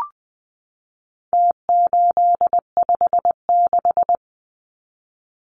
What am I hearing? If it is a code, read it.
Morse code, heard as a keyed tone, about 20 wpm, 700 Hz. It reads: T856
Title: Numbers - Contest Style - 4-Digit 20wpm